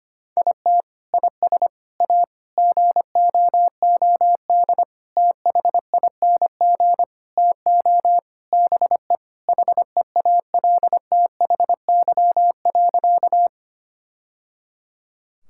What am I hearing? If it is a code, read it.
Morse code, heard as a keyed tone, about 25 wpm, 700 Hz. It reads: IT IS A GOOD THING TO BE HEALTHY.
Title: Sentences from Top 1000 Words - Review Part 5 - 25wpm